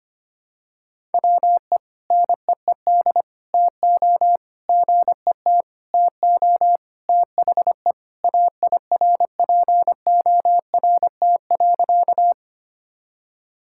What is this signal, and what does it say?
Morse code, heard as a keyed tone, about 25 wpm, 700 Hz. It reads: WE NEED TO GET TO THE AIRPORT.
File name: practice / 25wpm / Sentences with 3-Word Phrases - 25wpm